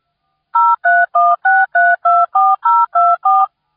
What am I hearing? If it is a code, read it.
DTMF, heard as telephone touch tones, tones of about 220 ms. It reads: 0316324024